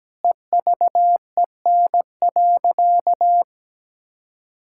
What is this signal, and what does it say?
Morse code, heard as a keyed tone, about 17 wpm, 700 Hz. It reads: EVEN.
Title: Sentences from Top 1600 Words 17wpm